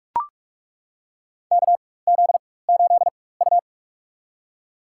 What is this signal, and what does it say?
Morse code, heard as a keyed tone, about 41 wpm, spaced as for 20 wpm, 700 Hz. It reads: KZ8U
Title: USA Call Signs - 4 Characters - 4x Character Spacing - 45wpm